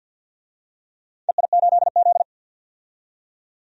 Morse code, heard as a keyed tone, about 47 wpm, 700 Hz. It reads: EI8Z
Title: DXCC-100 Call Signs - 4 Characters - 50wpm